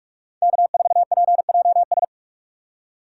Morse code, heard as a keyed tone, about 44 wpm, 700 Hz. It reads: K4PJS